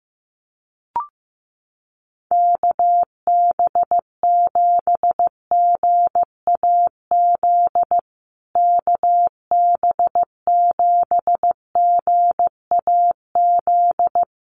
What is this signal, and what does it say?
Morse code, heard as a keyed tone, about 15 wpm, 700 Hz. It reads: KB7GAZ KB7GAZ
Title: USA Call Signs Repeated 2x 15wpm